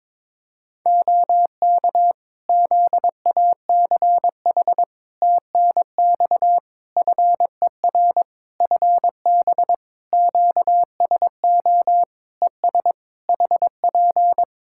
Morse code, heard as a keyed tone, about 22 wpm, 700 Hz. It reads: OK ZACH TNX FER FB QSO ES HP
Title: K1USN SST QSO Practice 22wpm